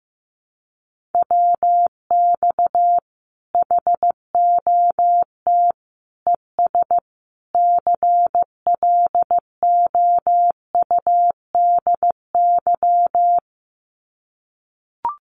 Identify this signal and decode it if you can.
Morse code, heard as a keyed tone, about 15 wpm, 700 Hz. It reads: WX HOT ES CLOUDY
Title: Common QSO Phrases 15wpm